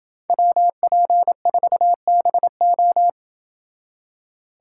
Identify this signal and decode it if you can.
Morse code, heard as a keyed tone, about 27 wpm, 700 Hz. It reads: WP4BO